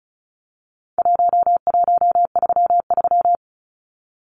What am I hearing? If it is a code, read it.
Morse code, heard as a keyed tone, about 34 wpm, 700 Hz. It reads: 1133
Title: Numbers - 4-Digit 35wpm